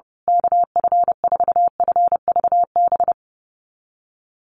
Morse code, heard as a keyed tone, about 30 wpm, 700 Hz. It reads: KF4FVB
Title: USA Call Signs Repeated 3x 30wpm